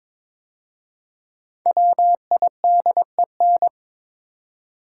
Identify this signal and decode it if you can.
Morse code, heard as a keyed tone, about 22 wpm, 700 Hz. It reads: WIDEN
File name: 5 Letter Words - First 3 Letters Given - 22wpm